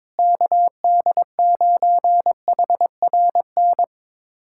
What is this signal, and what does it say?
Morse code, heard as a keyed tone, about 22 wpm, 700 Hz. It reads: KD9HRN
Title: USA Callsign DE Callsign 22wpm